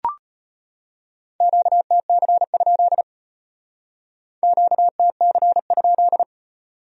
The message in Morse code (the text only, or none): QTC? QTC?